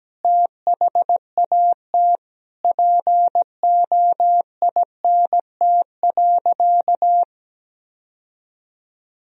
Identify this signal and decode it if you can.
Morse code, heard as a keyed tone, about 17 wpm, 700 Hz. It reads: THAT POINT.
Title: Sentences from Top 1000 Words - Review Part 1 - 17wpm